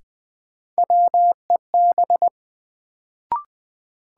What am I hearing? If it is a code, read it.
Morse code, heard as a keyed tone, about 20 wpm, 700 Hz. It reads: WEB
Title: Sentences from Top 1000 Words - Review Part 4 - New Word Drill - 20wpm